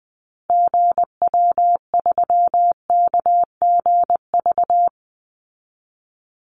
Morse code, heard as a keyed tone, about 20 wpm, 700 Hz. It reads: GW3KGV
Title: Call Signs Repeated 3x - 20wpm